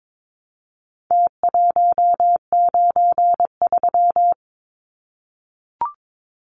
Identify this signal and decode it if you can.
Morse code, heard as a keyed tone, about 22 wpm, 700 Hz. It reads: T193
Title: Numbers - Contest Style - 4-Digit 22wpm